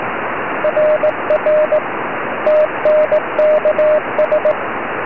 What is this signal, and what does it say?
Morse code, heard as a keyed tone, about 18 wpm, 610 Hz, with radio noise. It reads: RR TNKS
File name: ei2glb 4m